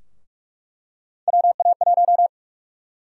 Morse code, heard as a keyed tone, about 46 wpm, 700 Hz. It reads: WA1
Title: USA Call Sign Prefixes - Repeated 2x - 45wpm